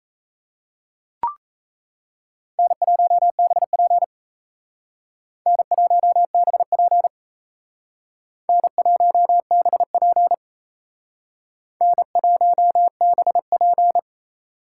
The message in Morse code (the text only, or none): N1BP N1BP N1BP N1BP